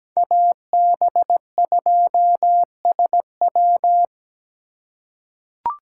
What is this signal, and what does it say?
Morse code, heard as a keyed tone, about 17 wpm, 700 Hz. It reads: AB2SW
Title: USA Call Signs - 5 Characters - 3x Character Spacing - 17wpm